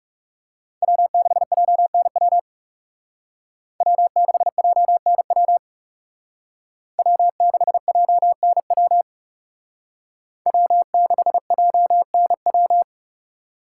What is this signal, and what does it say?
Morse code, heard as a keyed tone, about 37 wpm, 700 Hz. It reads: W6JNW W6JNW W6JNW W6JNW
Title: Speed Racing - USA 1x3 Call Signs 30wpm